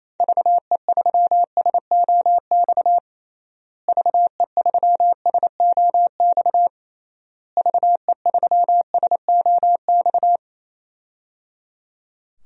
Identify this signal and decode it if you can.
Morse code, heard as a keyed tone, about 28 wpm, 700 Hz. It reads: VE3SOX VE3SOX VE3SOX
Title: DXCC-100 Call Signs - 6 Characters - Repeated 3x - 28wpm